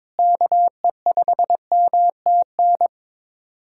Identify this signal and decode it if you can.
Morse code, heard as a keyed tone, about 22 wpm, 700 Hz. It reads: KE5MTN
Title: USA Callsign DE Callsign - 6 Characters - 22wpm